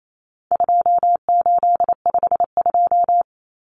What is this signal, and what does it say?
Morse code, heard as a keyed tone, about 28 wpm, 700 Hz. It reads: 2852
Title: Common Cut Numbers - 4-Digit 28wpm